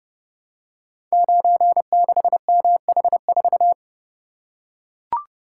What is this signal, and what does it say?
Morse code, heard as a keyed tone, about 30 wpm, 700 Hz. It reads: 96MH4
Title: Code Groups - 5 Letter:Numbers - 30wpm